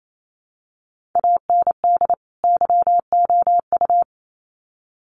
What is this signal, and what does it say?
Morse code, heard as a keyed tone, about 28 wpm, 700 Hz. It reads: AND YOU
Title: Sets of 2 Words 28wpm